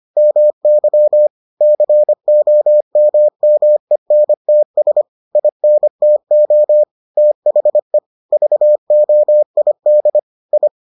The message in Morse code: MY COMMENTS INTO THE VOID I

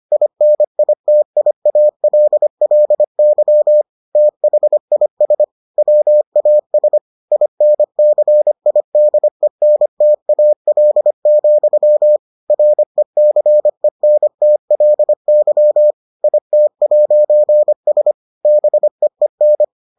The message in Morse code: INITIALLY THIS WAS INCIDENTAL, RECENTLY IT'S BEEN

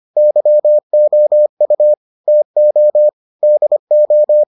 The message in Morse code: YOU TO DO

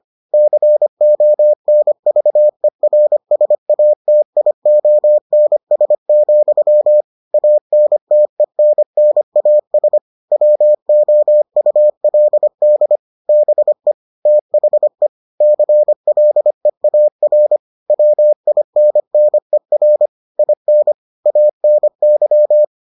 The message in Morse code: CONVERSATIONS, ANTENNAS WOULD BE THE CLEAR WINNER IN ANY

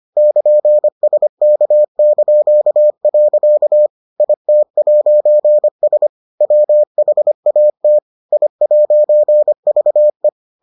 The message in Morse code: <KN>SK). IT'S WHAT I'VE

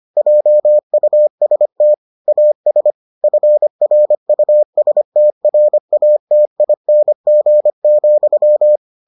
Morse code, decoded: JUST AS FRUSTRATING,

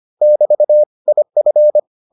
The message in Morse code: <BT> IF

